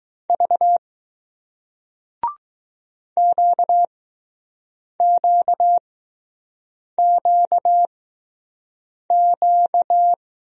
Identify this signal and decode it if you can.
Morse code, heard as a keyed tone, about 20 wpm, spaced as for 6 wpm, 700 Hz. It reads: V QQQQ